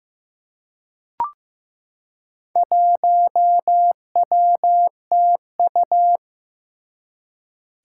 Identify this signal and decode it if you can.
Morse code, heard as a keyed tone, about 16 wpm, 700 Hz. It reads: 1WTU
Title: Four Challenging Letter-Number Combinations 15wpm